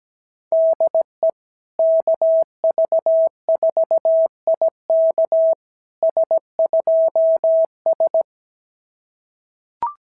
Morse code, heard as a keyed tone, about 17 wpm, 650 Hz. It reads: DE KV4IK S2S
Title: SOTA QSO Practice 17wpm